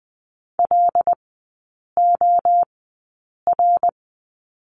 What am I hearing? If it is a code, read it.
Morse code, heard as a keyed tone, about 20 wpm, spaced as for 8 wpm, 700 Hz. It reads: LOR
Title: 074 Lesson 13e - Letter L - Three Characters 20wpm